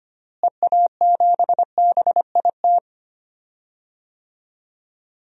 Morse code, heard as a keyed tone, about 25 wpm, 700 Hz. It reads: EA7BIT
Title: DXCC-100 Call Signs - 6 Characters - Repeated 2x - 25wpm